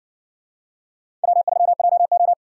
Morse code, heard as a keyed tone, about 59 wpm, 700 Hz. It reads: W3JO